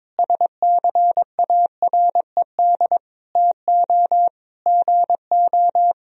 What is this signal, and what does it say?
Morse code, heard as a keyed tone, about 22 wpm, 700 Hz. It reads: SCARED TO GO
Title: Sentences from Top 1000 Words - Review Part 4 - 22wpm